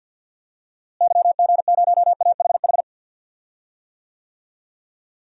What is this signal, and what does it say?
Morse code, heard as a keyed tone, about 46 wpm, 700 Hz. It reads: YG0AHH
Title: DXCC-100 Call Signs - 6 Characters - Repeated 3x - 50wpm